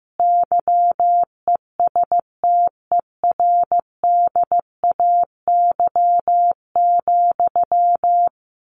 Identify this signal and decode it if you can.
Morse code, heard as a keyed tone, about 15 wpm, 700 Hz. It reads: YESTERDAY,
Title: A Sound Of Thunder 15wpm